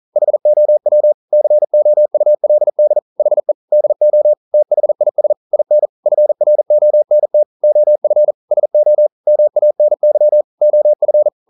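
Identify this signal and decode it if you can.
Morse code, heard as a keyed tone, about 41 wpm, 600 Hz. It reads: HOW COULD HE DO THIS IN FRONT OF SO MANY OF
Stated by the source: Morse code practice at 41 wpm, from CW med Gnister.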